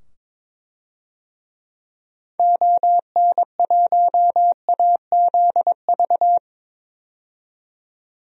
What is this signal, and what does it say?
Morse code, heard as a keyed tone, about 22 wpm, 700 Hz. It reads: ON1AZV
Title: DXCC-100 Call Signs - 6 Characters - 22wpm